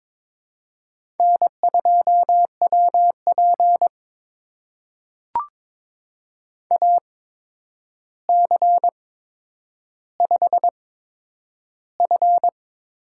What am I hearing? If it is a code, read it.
Morse code, heard as a keyed tone, about 22 wpm, spaced as for 9 wpm, 700 Hz. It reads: N2WP AC5F
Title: USA Call Signs - 4 Characters - 8x Character Spacing - 22wpm